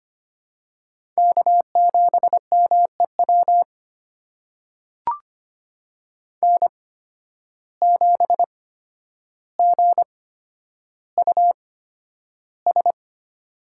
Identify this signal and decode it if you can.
Morse code, heard as a keyed tone, about 25 wpm, spaced as for 11 wpm, 700 Hz. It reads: K7MEW N7GUS
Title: USA Call Signs - 5 Characters - 8x Character Spacing - 25wpm